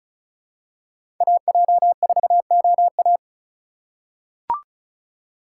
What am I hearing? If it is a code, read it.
Morse code, heard as a keyed tone, about 36 wpm, 700 Hz. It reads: AJ4OA
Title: USA Call Signs - 4x Character Spacing - 35wpm